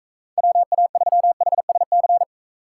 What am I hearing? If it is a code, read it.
Morse code, heard as a keyed tone, about 43 wpm, 700 Hz. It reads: WA3HSC